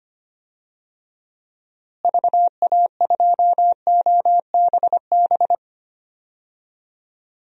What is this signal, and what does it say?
Morse code, heard as a keyed tone, about 25 wpm, 700 Hz. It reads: VA2OBB
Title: DXCC-100 Call Signs - 6 Characters - Repeated 3x - 25wpm